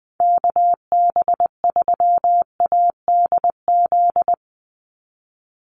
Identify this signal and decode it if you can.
Morse code, heard as a keyed tone, about 20 wpm, 700 Hz. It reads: KB3ADZ
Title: USA Call Signs 20wpm